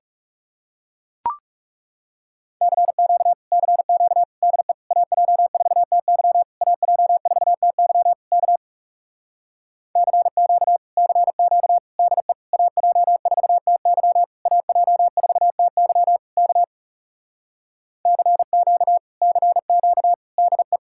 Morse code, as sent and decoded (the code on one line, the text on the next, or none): -.-. --.- / -.-. --.- / -.. / .- .--- ....- - -.-- / .- .--- ....- - -.-- / -.- / -.-. --.- / -.-. --.- / -.. / .- .--- ....- - -.-- / .- .--- ....- - -.-- / -.- / -.-. --.- / -.-. --.- / -.. .
CQ CQ DE AJ4TY AJ4TY K CQ CQ DE AJ4TY AJ4TY K CQ CQ DE